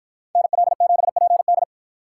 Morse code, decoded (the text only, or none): NL7PD